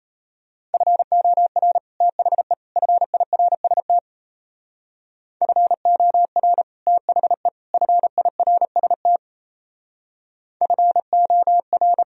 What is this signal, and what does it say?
Morse code, heard as a keyed tone, about 34 wpm, 700 Hz. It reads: FOR THE FIRST FOR THE FIRST FOR
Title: Speed-Racing - Sets of 3 Words 28wpm